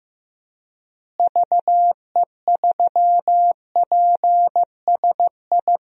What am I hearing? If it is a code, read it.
Morse code, heard as a keyed tone, about 15 wpm, 700 Hz. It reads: VE3PSI
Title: DXCC-100 Call Signs - 6 Characters - Repeated 2x - 15wpm